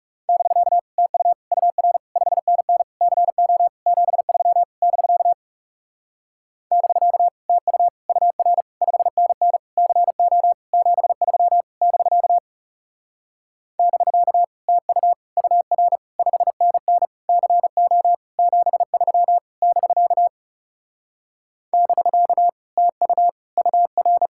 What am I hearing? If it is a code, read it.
Morse code, heard as a keyed tone, about 40 wpm, 700 Hz. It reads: <BK> TU UR 5NN CO 73 <BK> <BK> TU UR 5NN CO 73 <BK> <BK> TU UR 5NN CO 73 <BK> <BK> TU UR